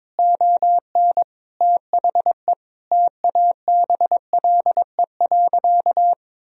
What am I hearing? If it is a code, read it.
Morse code, heard as a keyed tone, about 22 wpm, 700 Hz. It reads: ON THE TABLE.
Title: Sentences from Top 1000 Words - Review Part 2 - 22wpm